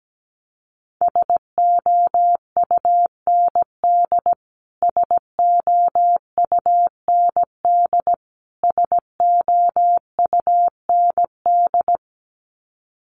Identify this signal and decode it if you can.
Morse code, heard as a keyed tone, about 17 wpm, 700 Hz. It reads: SOUND SOUND SOUND
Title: Top 200 Words Repeated 17wpm